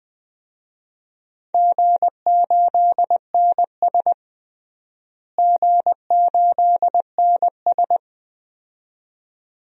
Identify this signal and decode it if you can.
Morse code, heard as a keyed tone, about 21 wpm, 700 Hz. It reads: G8NS G8NS
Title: DXCC-100 Call Signs - 4 Characters - Repeated 2x - 20wpm